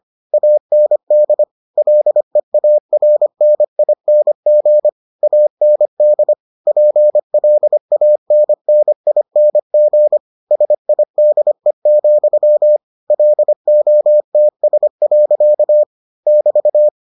Morse code, heard as a keyed tone, about 25 wpm, 600 Hz. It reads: AND LEARNING AND PLANNING SIDE, LOTS. <BT>